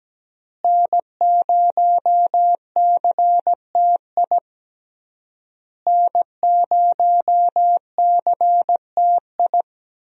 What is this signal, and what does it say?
Morse code, heard as a keyed tone, about 17 wpm, 700 Hz. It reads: N0CTI N0CTI
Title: USA Call Signs - 5 Characters - Repeated 3x - 17wpm